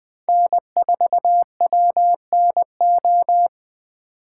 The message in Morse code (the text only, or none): N4WNO